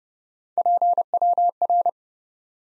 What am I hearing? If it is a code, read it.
Morse code, heard as a keyed tone, about 30 wpm, 700 Hz. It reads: PWR
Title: Top 90 Abbr 30WPM